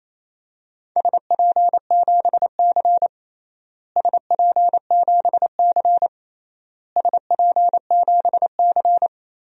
Morse code, heard as a keyed tone, about 28 wpm, 700 Hz. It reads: SP7C SP7C SP7C
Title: DXCC-100 Call Signs - 4 Characters - Repeated 3x - 28wpm